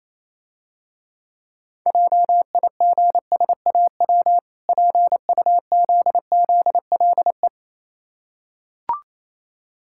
Morse code, heard as a keyed tone, about 28 wpm, 700 Hz. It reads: JIGSAW PUZZLE
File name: Easy Words - Speed-Racing with Hints - 22wpm